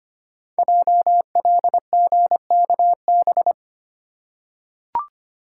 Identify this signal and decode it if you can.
Morse code, heard as a keyed tone, about 25 wpm, 700 Hz. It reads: JLGKB
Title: Code Groups - 5 Letters - 25wpm